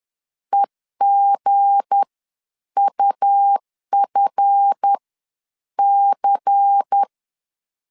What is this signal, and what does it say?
Morse code, heard as a keyed tone, about 10 wpm, spaced as for 8 wpm, 800 Hz. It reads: EGUFC
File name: LEZIONI / 7 GF9Z / 7 GF 10wpm 2514